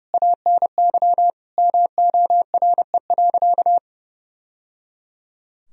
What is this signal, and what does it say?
Morse code, heard as a keyed tone, about 30 wpm, 700 Hz. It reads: ANY MORE.
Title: Sentences from Top 1000 Words - Review Part 1 - 30wpm